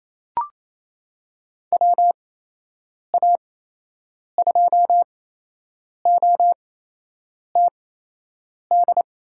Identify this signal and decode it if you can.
Morse code, heard as a keyed tone, about 27 wpm, spaced as for 8 wpm, 700 Hz. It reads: WA2OTD
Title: USA Call Signs - 8x Character Spacing - 28wpm